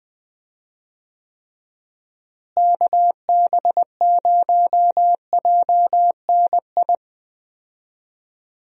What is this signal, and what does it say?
Morse code, heard as a keyed tone, about 20 wpm, 700 Hz. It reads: KB0JNI
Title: USA Call Signs - 6 Characters - Repeated 2x - 20wpm